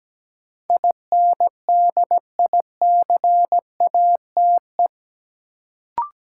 Morse code, heard as a keyed tone, about 17 wpm, 700 Hz. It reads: INDICATE
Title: Sentences from Top 1300 Words - New Word Drill - 17wpm